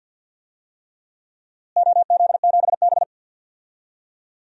Morse code, heard as a keyed tone, about 50 wpm, 700 Hz. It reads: OZ7B
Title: DXCC-100 Call Signs - 4 Characters - Repeated 3x - 50wpm